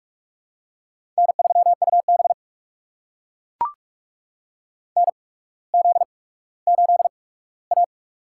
Morse code, heard as a keyed tone, about 44 wpm, spaced as for 19 wpm, 700 Hz. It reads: N3UB NZ8A
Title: USA Call Signs - 4 Characters - 8x Character Spacing - 45wpm